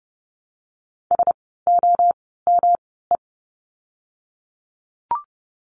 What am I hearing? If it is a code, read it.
Morse code, heard as a keyed tone, about 30 wpm, spaced as for 17 wpm, 700 Hz. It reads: SOME
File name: course / 30:10wpm / 108 Lesson 18f - Letter m - Words 30wpm